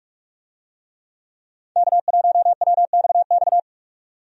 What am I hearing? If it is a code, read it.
Morse code, heard as a keyed tone, about 46 wpm, 700 Hz. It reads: K1WXX
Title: USA Call Signs - 5 Characters - 45wpm